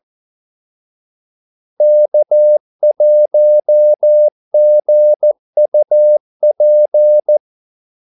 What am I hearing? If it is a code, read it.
Morse code, heard as a keyed tone, about 14 wpm, 600 Hz. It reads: K1GUP
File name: Callsigns-14wpm